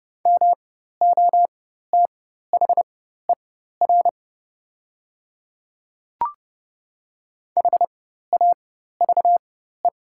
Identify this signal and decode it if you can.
Morse code, heard as a keyed tone, about 30 wpm, spaced as for 14 wpm, 700 Hz. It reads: MOTHER HAVE
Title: 4x Character Spacing - Top 300 Words - 30wpm